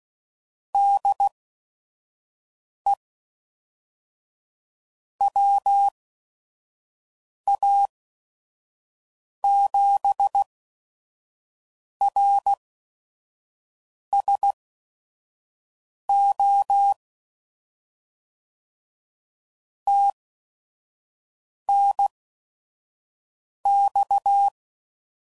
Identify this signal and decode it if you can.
Morse code, heard as a keyed tone, about 16 wpm, spaced as for 5 wpm, 800 Hz. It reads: DE WA7RSO TNX
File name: CW Code Groups @5WPM / QSOs / kc7haa